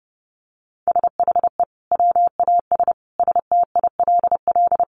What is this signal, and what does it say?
Morse code, heard as a keyed tone, about 30 wpm, 700 Hz. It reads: SHE WAS STILL
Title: Sentences from Top 500 Words 30wpm